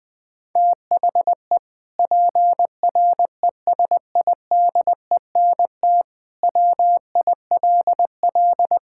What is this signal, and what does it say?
Morse code, heard as a keyed tone, about 20 wpm, 700 Hz. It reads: THE PRESIDENT WILL
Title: Sentences with 2-Word Phrases - 20wpm